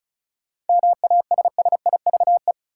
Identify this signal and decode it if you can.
Morse code, heard as a keyed tone, about 36 wpm, 700 Hz. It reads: MASSIVE
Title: Seven Letter Words - 35wpm